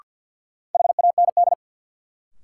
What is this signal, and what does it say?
Morse code, heard as a keyed tone, about 53 wpm, 700 Hz. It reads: HAND
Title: Sentences from Top 200 Words - New Word Drill - 50wpm